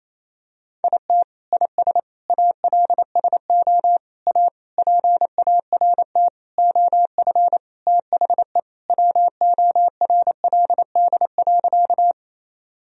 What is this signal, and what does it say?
Morse code, heard as a keyed tone, about 28 wpm, 700 Hz. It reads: IT IS ALSO A PART OF THE WORLD.
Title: Sentences from Top 1000 Words - Review Part 1 - 28wpm